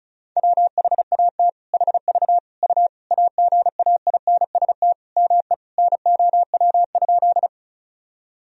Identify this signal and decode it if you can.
Morse code, heard as a keyed tone, about 36 wpm, 700 Hz. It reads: WHAT HV U AGAINST ME NOW?
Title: Sentences from Top 400 Words and Abbreviations 35wpm